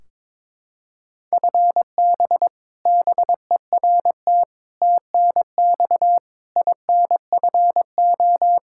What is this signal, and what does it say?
Morse code, heard as a keyed tone, about 22 wpm, 700 Hz. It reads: FB BERT TNX INFO
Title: Practice QSO - Exchange 3 - More Info - 22wpm